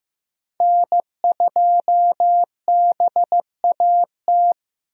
Morse code, heard as a keyed tone, about 15 wpm, 700 Hz. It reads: N2BAT
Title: USA Call Signs - 5 Characters - Repeated 2x - 15wpm